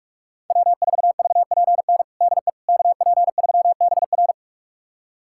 Morse code, heard as a keyed tone, about 41 wpm, 700 Hz. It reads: W4VPN DE KP3BR
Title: USA Callsign DE Callsign - 5 Characters - 45wpm